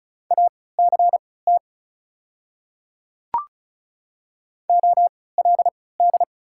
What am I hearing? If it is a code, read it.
Morse code, heard as a keyed tone, about 36 wpm, spaced as for 20 wpm, 700 Hz. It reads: ACT OLD
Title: 3x Character Spacing - Top 200 Words - 35wpm